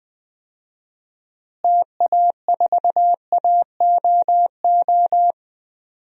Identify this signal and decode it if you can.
Morse code, heard as a keyed tone, about 21 wpm, 700 Hz. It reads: TA4AOO
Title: DXCC-100 Call Signs - 6 Characters - Repeated 3x - 20wpm